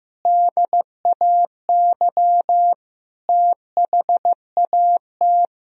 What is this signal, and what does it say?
Morse code, heard as a keyed tone, about 15 wpm, 700 Hz. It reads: DAY THAT
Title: Sentences from Top 1000 Words - Review Part 2 - 15wpm